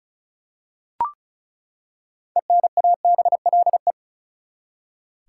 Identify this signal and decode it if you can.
Morse code, heard as a keyed tone, about 36 wpm, 700 Hz. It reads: ENABLE
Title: Six Letter Words - 35wpm